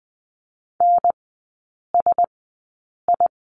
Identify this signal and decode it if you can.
Morse code, heard as a keyed tone, about 20 wpm, spaced as for 8 wpm, 700 Hz. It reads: NSI
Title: 074 Lesson 13e - Letter L - Three Characters 20wpm